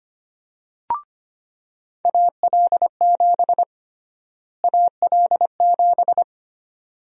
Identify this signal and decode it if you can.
Morse code, heard as a keyed tone, about 25 wpm, 700 Hz. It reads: AL7 AL7
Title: USA Call Sign Prefixes - Repeated 2x - 25wpm